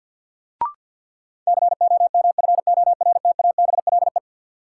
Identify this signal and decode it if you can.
Morse code, heard as a keyed tone, about 51 wpm, 700 Hz. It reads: COMFORTABLE
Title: Sentences from Top 1000 Words - Review Part 7 - New Word Drill - 50wpm